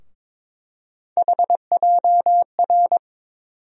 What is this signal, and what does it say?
Morse code, heard as a keyed tone, about 21 wpm, 700 Hz. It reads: HJR
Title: USA Call Sign Postfixes - 3 Letters 22wpm